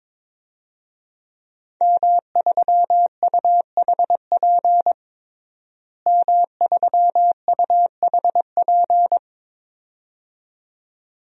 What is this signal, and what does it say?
Morse code, heard as a keyed tone, about 22 wpm, 700 Hz. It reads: M3UHP M3UHP
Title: DXCC-100 Call Signs - 5 Characters - Repeated 2x - 22wpm